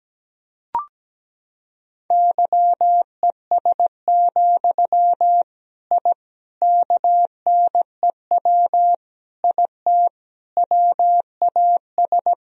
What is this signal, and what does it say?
Morse code, heard as a keyed tone, about 17 wpm, 700 Hz. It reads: YES, I KNEW IT WAS
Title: Sentences from Top 1000 Words - Review Part 2 - 17wpm